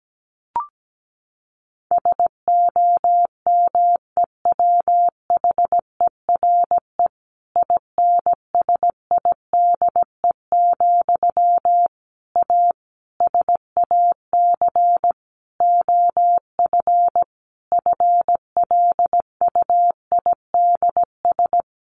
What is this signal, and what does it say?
Morse code, heard as a keyed tone, about 17 wpm, 700 Hz. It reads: SOMEWHERE INSIDE, A SAC OF FLUIDS
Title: A Sound Of Thunder 17wpm